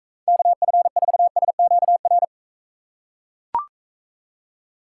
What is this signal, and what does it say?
Morse code, heard as a keyed tone, about 43 wpm, 700 Hz. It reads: KF4SQR